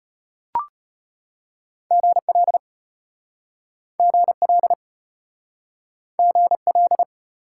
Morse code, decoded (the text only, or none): GL GL GL